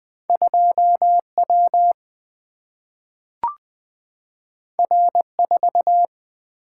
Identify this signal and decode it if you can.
Morse code, heard as a keyed tone, about 20 wpm, 700 Hz. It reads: 2W R4
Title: Single-Letter Single-Number 20wpm